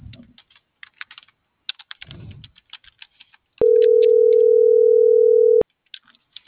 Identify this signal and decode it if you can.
Ringback tone (North America), 440 + 480 Hz.